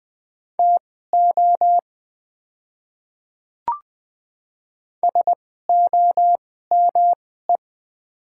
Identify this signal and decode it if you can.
Morse code, heard as a keyed tone, about 20 wpm, spaced as for 15 wpm, 700 Hz. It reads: TO SOME
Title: 2x Character Spacing - Top 100 Words - 20wpm